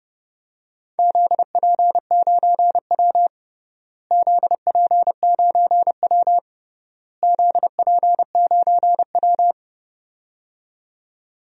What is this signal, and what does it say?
Morse code, heard as a keyed tone, about 30 wpm, 700 Hz. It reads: ZP9W ZP9W ZP9W
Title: DXCC-100 Call Signs - 4 Characters - Repeated 3x - 30wpm